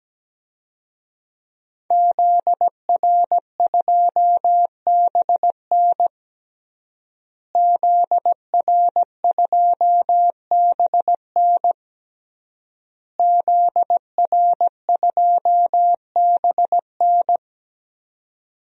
Morse code, heard as a keyed tone, about 17 wpm, 700 Hz. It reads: ZR2BN ZR2BN ZR2BN
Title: DXCC-100 Call Signs - 5 Characters - Repeated 3x - 17wpm